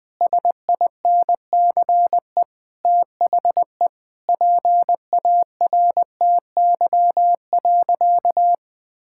Morse code, heard as a keyed tone, about 20 wpm, 700 Hz. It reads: SINCE THE PARTY.
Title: Sentences with 2-Word Phrases - 20wpm